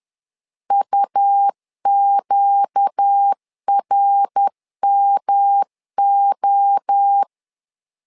Morse code, heard as a keyed tone, about 10 wpm, 800 Hz. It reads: UQRMO